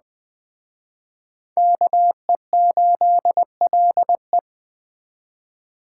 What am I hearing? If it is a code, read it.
Morse code, heard as a keyed tone, about 20 wpm, 700 Hz. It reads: KE8LE